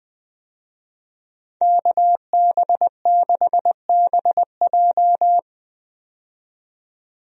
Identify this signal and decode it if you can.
Morse code, heard as a keyed tone, about 20 wpm, 700 Hz. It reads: KB6BJ
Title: USA Call Signs - 5 Characters - 20wpm